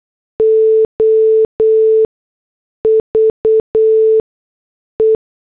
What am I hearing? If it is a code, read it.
Morse code, heard as a keyed tone, about 8 wpm, 440 Hz. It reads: O V E
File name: morse / headlines